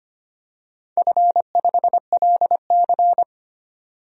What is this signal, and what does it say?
Morse code, heard as a keyed tone, about 25 wpm, 700 Hz. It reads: F5LC
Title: DXCC-100 Call Signs - 4 Characters - 25wpm